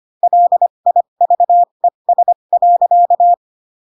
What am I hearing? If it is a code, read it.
Morse code, heard as a keyed tone, about 25 wpm, spaced as for 22 wpm, 700 Hz. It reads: LIVES.
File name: Pinocchio / Pinocchio ch2 18 25 ik1hge